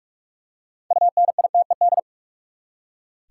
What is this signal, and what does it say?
Morse code, heard as a keyed tone, about 46 wpm, 700 Hz. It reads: UNITED
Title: Six Letter Words - 45wpm